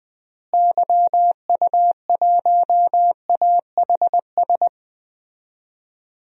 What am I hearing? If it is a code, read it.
Morse code, heard as a keyed tone, about 20 wpm, 700 Hz. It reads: YU1AHS
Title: DXCC-100 Call Signs - 6 Characters - Repeated 3x - 20wpm